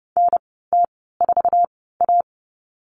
Morse code, heard as a keyed tone, about 30 wpm, spaced as for 17 wpm, 700 Hz. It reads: NT4A
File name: Lesson 19g - Letter w - Call Signs 30wpm